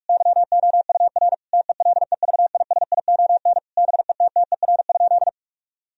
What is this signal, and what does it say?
Morse code, heard as a keyed tone, about 46 wpm, 700 Hz. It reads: YOUR TELEVISION BETTER?
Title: Sentences from Top 1000 Words - Review Part 4 - 45wpm